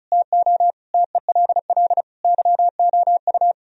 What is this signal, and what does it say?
Morse code, heard as a keyed tone, about 43 wpm, 700 Hz. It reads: TO TELL YOU
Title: Speed-Racing - Sets of 5 Words 35wpm